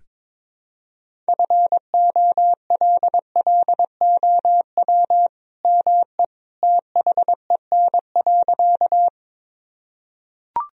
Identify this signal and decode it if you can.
Morse code, heard as a keyed tone, about 22 wpm, 700 Hz. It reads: FOLLOW ME THEN.
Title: Sentences from Top 1000 Words - Review Part 1 - 22wpm